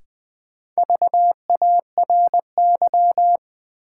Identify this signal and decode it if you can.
Morse code, heard as a keyed tone, about 20 wpm, 700 Hz. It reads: VARY
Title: Sentences from Top 1200 Words - New Word Drill - 20wpm